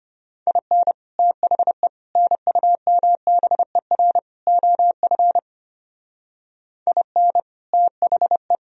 Morse code, heard as a keyed tone, about 32 wpm, 700 Hz. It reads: IN THE NUMBER OF IN THE